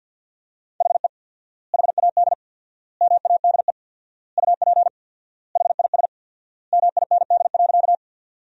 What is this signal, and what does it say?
Morse code, heard as a keyed tone, about 50 wpm, 700 Hz. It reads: HE HAD MADE UP HIS MIND.
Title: 4x Word-Spacing - Sentences from Top 500 Words - 50wpm